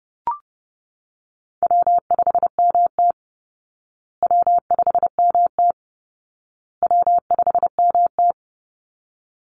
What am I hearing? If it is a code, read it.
Morse code, heard as a keyed tone, about 30 wpm, 700 Hz. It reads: W5MT W5MT W5MT
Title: Easy Call Signs Repeated 3x - 30wpm